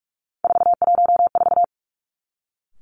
Morse code, heard as a keyed tone, about 46 wpm, 700 Hz. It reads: 414
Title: Three Challenging Letter-Number Combinations 45wpm